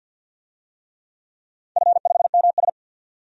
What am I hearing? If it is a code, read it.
Morse code, heard as a keyed tone, about 49 wpm, 700 Hz. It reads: F5MS
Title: DXCC-100 Call Signs - 4 Characters - 50wpm